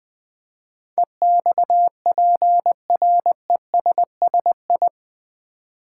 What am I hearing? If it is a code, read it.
Morse code, heard as a keyed tone, about 20 wpm, 700 Hz. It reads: EXPRESSI
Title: Build A Word - 4 Letters At A Time - 10-Letter Words - 20wpm